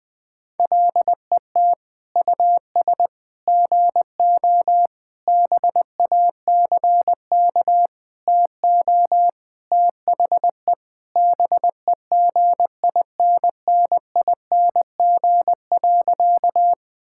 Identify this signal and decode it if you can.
Morse code, heard as a keyed tone, about 19 wpm, 700 Hz. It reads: LET US GO BACK TO THE BEGINNING.
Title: Sentences with 4-Word Phrases - Spoken Beginning - 20wpm